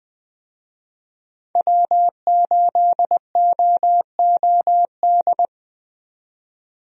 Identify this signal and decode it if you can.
Morse code, heard as a keyed tone, about 20 wpm, 700 Hz. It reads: W8OOD
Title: USA Call Signs - 5 Characters - 20wpm